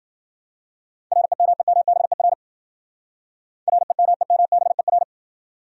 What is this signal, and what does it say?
Morse code, heard as a keyed tone, about 52 wpm, 700 Hz. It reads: REMEMBER REMEMBER